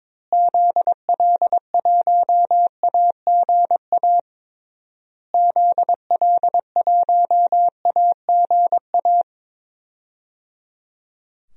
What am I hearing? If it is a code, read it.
Morse code, heard as a keyed tone, about 21 wpm, 700 Hz. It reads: ZL1AGA ZL1AGA
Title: DXCC-100 Call Signs - 6 Characters - Repeated 3x - 22wpm